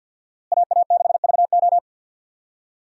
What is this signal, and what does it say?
Morse code, heard as a keyed tone, about 52 wpm, 700 Hz. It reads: AA6VO
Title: USA Call Signs - 5 Characters - Repeated 2x - 50wpm